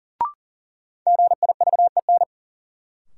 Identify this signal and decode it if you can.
Morse code, heard as a keyed tone, about 38 wpm, 700 Hz. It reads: GIVEN
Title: Sentences from Top 1400 Words - New Word Drill Alt - 40wpm